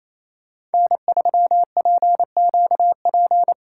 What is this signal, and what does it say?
Morse code, heard as a keyed tone, about 36 wpm, 700 Hz. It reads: N3PQP